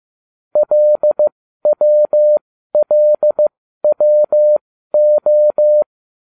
Morse code, heard as a keyed tone, about 15 wpm, spaced as for 12 wpm, 600 Hz. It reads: LWLWO